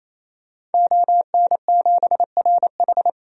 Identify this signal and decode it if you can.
Morse code, heard as a keyed tone, about 28 wpm, 700 Hz. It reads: ON7RH
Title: DXCC-100 Call Signs - 5 Characters - 28wpm